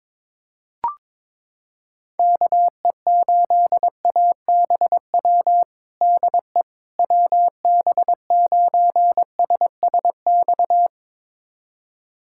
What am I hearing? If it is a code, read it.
Morse code, heard as a keyed tone, about 22 wpm, 700 Hz. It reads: KE8ABW DE WB9SSX
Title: USA Callsign DE Callsign - 6 Characters - 22wpm